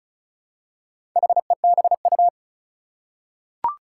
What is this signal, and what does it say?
Morse code, heard as a keyed tone, about 36 wpm, 700 Hz. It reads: HEBU
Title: Four Challenging Letter-Number Combinations 35wpm